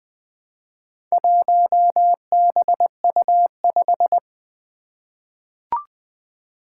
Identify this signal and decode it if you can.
Morse code, heard as a keyed tone, about 20 wpm, 700 Hz. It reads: 1BU5
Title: Four Challenging Letter-Number Combinations 20wpm